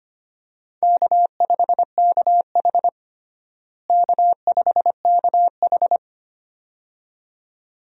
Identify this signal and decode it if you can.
Morse code, heard as a keyed tone, about 25 wpm, 700 Hz. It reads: K5KH K5KH
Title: USA Call Signs - 4 Characters - Repeated 2x - 25wpm